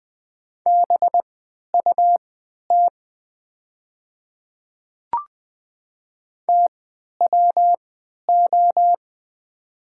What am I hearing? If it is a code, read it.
Morse code, heard as a keyed tone, about 20 wpm, spaced as for 11 wpm, 700 Hz. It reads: BUT TWO
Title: 3x Character Spacing - Top 200 Words - 20wpm